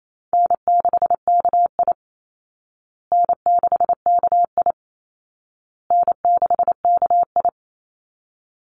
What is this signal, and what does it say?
Morse code, heard as a keyed tone, about 28 wpm, 700 Hz. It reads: N6KI N6KI N6KI
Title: Easy Call Signs Repeated 3x - 28wpm